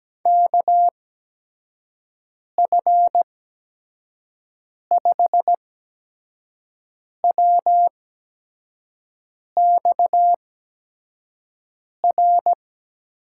Morse code, heard as a keyed tone, about 17 wpm, spaced as for 5 wpm, 700 Hz. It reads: KF5WXR
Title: USA Call Signs - 6 Characters - 8x Character Spacing - 17wpm